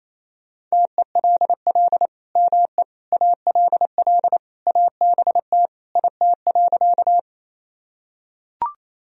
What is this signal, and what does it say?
Morse code, heard as a keyed tone, about 28 wpm, 700 Hz. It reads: TELL ME ALL ABT IT.
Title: Sentences from Top 200 Words and Abbreviations 28wpm